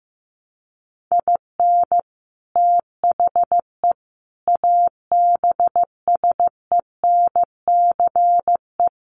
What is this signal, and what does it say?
Morse code, heard as a keyed tone, about 15 wpm, 700 Hz. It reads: IN THE ABSENCE
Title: Sets of 4 Words 15wpm